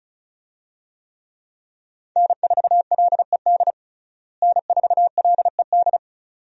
Morse code, heard as a keyed tone, about 36 wpm, 700 Hz. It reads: N4LED N4LED
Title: USA Call Signs - 5 Characters - Repeated 2x - 35wpm